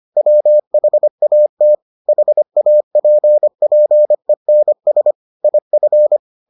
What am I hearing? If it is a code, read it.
Morse code, heard as a keyed tone, about 25 wpm, 600 Hz. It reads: WHAT HAPPENS IF